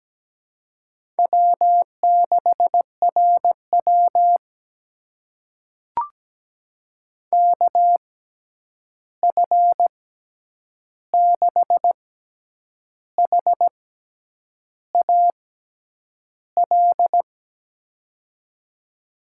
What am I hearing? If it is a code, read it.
Morse code, heard as a keyed tone, about 17 wpm, spaced as for 8 wpm, 700 Hz. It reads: W6RW KF6HAL
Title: USA Call Signs - 6x Character Spacing - 17wpm